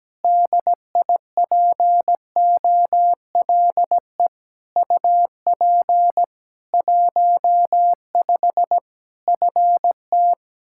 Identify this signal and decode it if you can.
Morse code, heard as a keyed tone, about 17 wpm, 700 Hz. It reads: DIPOLE UP 15 FT
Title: Practice QSO - Exchange 2 - Info - 17wpm